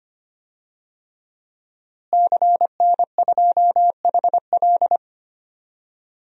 Morse code, heard as a keyed tone, about 25 wpm, 700 Hz. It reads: CN2HL
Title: DXCC-100 Call Signs - 5 Characters - Repeated 3x - 25wpm